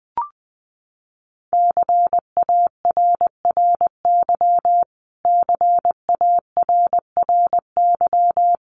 Morse code, heard as a keyed tone, about 20 wpm, 700 Hz. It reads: CARRY CARRY
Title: Top 500 Words Repeated 20wpm